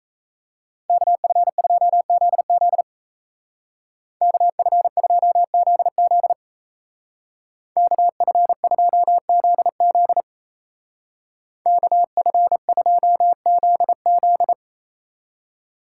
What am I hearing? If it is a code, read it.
Morse code, heard as a keyed tone, about 35 wpm, 700 Hz. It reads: KF2ZZ KF2ZZ KF2ZZ KF2ZZ
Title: Speed Racing - USA 2x2 Call Signs 28wpm